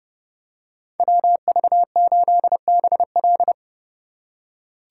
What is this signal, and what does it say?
Morse code, heard as a keyed tone, about 31 wpm, 700 Hz. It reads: WV8BL
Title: USA Call Signs - 5 Characters - 4x Character Spacing - 30wpm